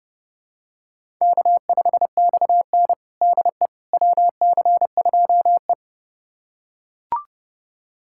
Morse code, heard as a keyed tone, about 30 wpm, 700 Hz. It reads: K5XN DE WC2E
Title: USA Callsign DE Callsign - 4 Characters - 30wpm